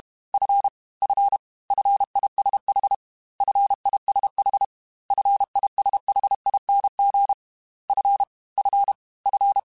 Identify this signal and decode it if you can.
Morse code, heard as a keyed tone, about 32 wpm, 800 Hz. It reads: F F FISH FISH FISHING F F F
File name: F and L / F and L mp3 / F and L Prefixes 3x2 Twice @30